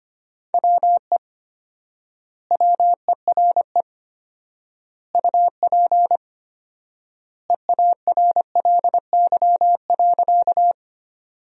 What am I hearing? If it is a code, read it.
Morse code, heard as a keyed tone, about 25 wpm, 700 Hz. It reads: WE WERE UP EARLY.
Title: 4x Word-Spacing - Sentences from Top 400 Words - 25wpm